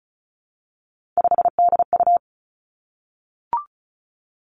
Morse code, heard as a keyed tone, about 36 wpm, 700 Hz. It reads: 5DU
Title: Three Challenging Letter-Number Combinations 35wpm